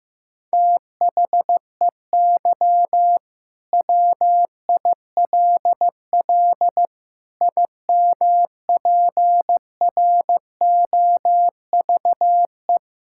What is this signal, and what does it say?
Morse code, heard as a keyed tone, about 15 wpm, 700 Hz. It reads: THEY WILL IMPROVE
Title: Sentences from Top 1000 Words - Review Part 5 - 15wpm